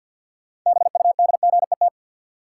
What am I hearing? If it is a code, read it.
Morse code, heard as a keyed tone, about 53 wpm, 700 Hz. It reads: BUDGET
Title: Sentences from Top 1200 Words - New Word Drill - 50wpm